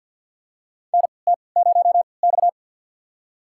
Morse code, heard as a keyed tone, about 48 wpm, spaced as for 28 wpm, 700 Hz. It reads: NT0X
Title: USA Call Signs - 4 Characters - 3x Character Spacing - 50wpm